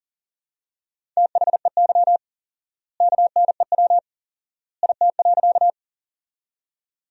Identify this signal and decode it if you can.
Morse code, heard as a keyed tone, about 40 wpm, 700 Hz. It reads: THEY KNEW IT.
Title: 4x Word-Spacing - Sentences from Top 400 Words - 40wpm